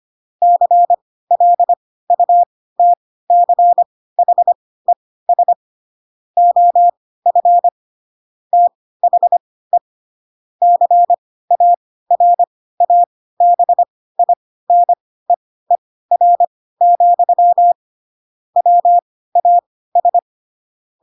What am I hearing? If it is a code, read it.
Morse code, heard as a keyed tone, about 25 wpm, spaced as for 16 wpm, 700 Hz. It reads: CLUTCHES OF THE CARABINEER, WAS
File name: Pinocchio / Pinocchio ch4 10 25 ik1hge